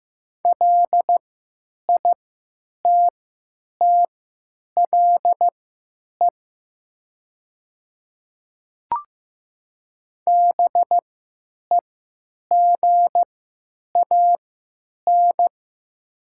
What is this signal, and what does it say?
Morse code, heard as a keyed tone, about 15 wpm, spaced as for 9 wpm, 700 Hz. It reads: LITTLE BEGAN
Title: 3x Character Spacing - Top 500 Words - 15wpm